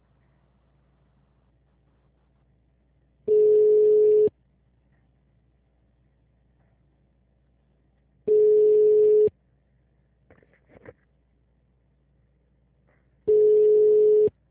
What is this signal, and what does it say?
Ringback tone (Europe, 425 Hz).
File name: out- -2002- - - .110